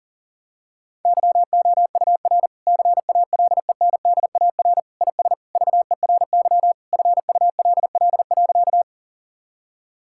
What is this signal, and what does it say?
Morse code, heard as a keyed tone, about 40 wpm, 700 Hz. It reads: YOUR CALENDAR IS VERY FULL.
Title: Sentences from Top 1600 Words 40wpm